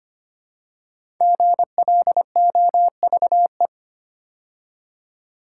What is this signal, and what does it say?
Morse code, heard as a keyed tone, about 25 wpm, 700 Hz. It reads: GLOVE
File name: 5 Letter Words - First 3 Letters Given - 25wpm